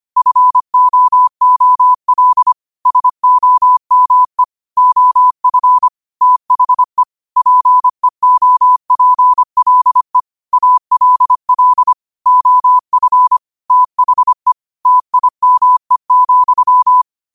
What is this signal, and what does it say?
Morse code, heard as a keyed tone, about 25 wpm, 1000 Hz. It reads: FOOL SOME OF THE PEOPLE ALL OF THE TIME,
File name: qotdfiles / 25wpm